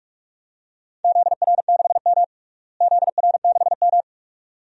Morse code, heard as a keyed tone, about 43 wpm, 700 Hz. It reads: ZR6M ZR6M
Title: DXCC-100 Call Signs - 4 Characters - Repeated 2x - 45wpm